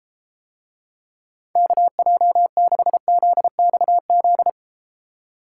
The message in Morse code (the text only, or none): KJ6ZXZ